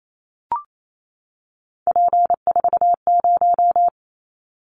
Morse code, heard as a keyed tone, about 28 wpm, 700 Hz. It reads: P40
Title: Callsign Prefixes for Top 100 DXCC - 28wpm